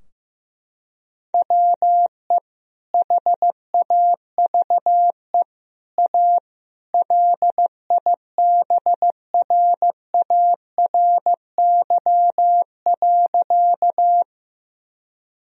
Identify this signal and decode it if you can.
Morse code, heard as a keyed tone, about 15 wpm, 700 Hz. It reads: WE HAVE A LIBRARY.
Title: Sentences from Top 1000 Words - Review Part 4 - 15wpm